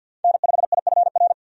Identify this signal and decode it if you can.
Morse code, heard as a keyed tone, about 51 wpm, 700 Hz. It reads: N5IFR
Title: USA Call Signs - 5 Characters - 50wpm